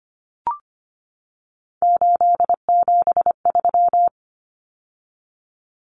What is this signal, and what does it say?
Morse code, heard as a keyed tone, about 25 wpm, 700 Hz. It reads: 873
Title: Numbers - 3-Digit 25wpm